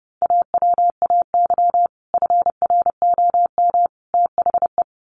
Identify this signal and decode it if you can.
Morse code, heard as a keyed tone, about 30 wpm, 700 Hz. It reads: AWAY FROM THE
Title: Sets of 3 Words 30wpm